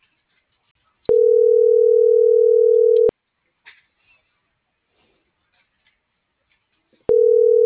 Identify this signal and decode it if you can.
Ringback tone (North America), 440 + 480 Hz.
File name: CENTRAL MORENA / out jacque.aguilera.25